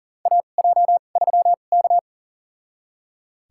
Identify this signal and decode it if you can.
Morse code, heard as a keyed tone, about 40 wpm, spaced as for 29 wpm, 700 Hz. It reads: AJ3K
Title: USA Call Signs - 2x Character Spacing - 40wpm